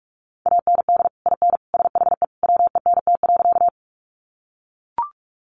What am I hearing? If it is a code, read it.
Morse code, heard as a keyed tone, about 44 wpm, 700 Hz. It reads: AND IN SHE WENT.
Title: Sentences from Top 200 Words 45wpm